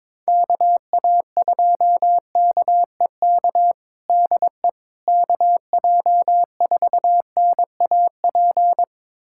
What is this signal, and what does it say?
Morse code, heard as a keyed tone, about 21 wpm, 700 Hz. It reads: KA2KEK DE KJ4NAP
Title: USA Callsign DE Callsign - 6 Characters - 22wpm